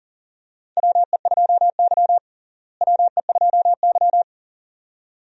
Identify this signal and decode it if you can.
Morse code, heard as a keyed tone, about 40 wpm, 700 Hz. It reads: WE2Y WE2Y
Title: USA Call Signs - 4 Characters - Repeated 2x - 40wpm